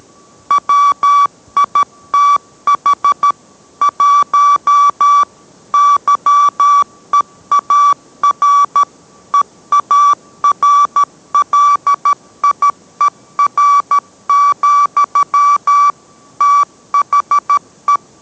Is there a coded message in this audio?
Yes: Morse code from 0.5 to 18.0 s, heard as a keyed tone, about 13 wpm, 1150 Hz. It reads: WITH 1 YEAR EARLIER, THE